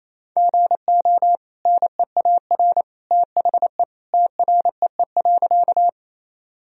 Morse code, heard as a keyed tone, about 28 wpm, 700 Hz. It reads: GO NEAR THE TREE.
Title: Sentences from Top 1000 Words - Review Part 1 - 28wpm